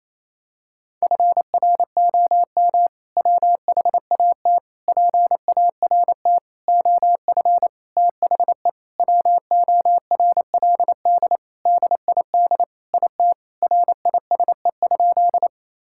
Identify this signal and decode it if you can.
Morse code, heard as a keyed tone, about 27 wpm, 700 Hz. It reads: FROM WHAT PART OF THE WORLD DID IT RISE?
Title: Sentences from Top 1000 Words - Review Part 6 - 28wpm